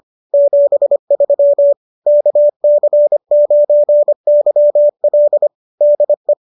Morse code, heard as a keyed tone, about 25 wpm, 590 Hz. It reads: 73 KC9YL DE